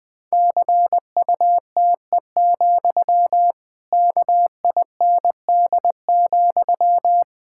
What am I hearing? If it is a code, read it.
Morse code, heard as a keyed tone, about 20 wpm, 700 Hz. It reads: CUTE, KIND,
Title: Pangram Challenge v2 - 20wpm